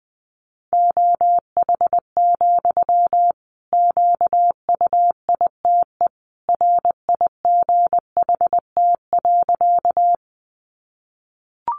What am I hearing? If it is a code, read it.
Morse code, heard as a keyed tone, about 20 wpm, 700 Hz. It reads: OH, QUITE RIGHT.
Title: Sentences from Top 700 Words 20wpm